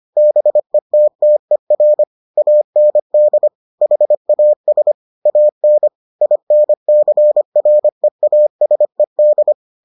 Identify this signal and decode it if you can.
Morse code, heard as a keyed tone, about 25 wpm, 600 Hz. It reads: BETTER AND HAS AN INCREASED